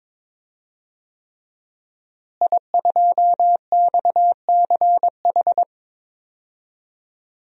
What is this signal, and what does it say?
Morse code, heard as a keyed tone, about 22 wpm, 700 Hz. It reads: I2XCH
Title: DXCC-100 Call Signs - 5 Characters - Repeated 3x - 22wpm